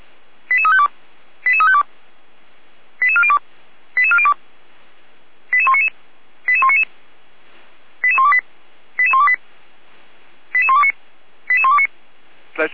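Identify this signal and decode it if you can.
Five-tone selective call (ZVEI), heard as quick runs of five tones, tones of about 70 ms. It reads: 88352 88352 88402 88402 88199 88199 88128 88128 88129 88129
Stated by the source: ZVEI five-tone selective call received by SDR.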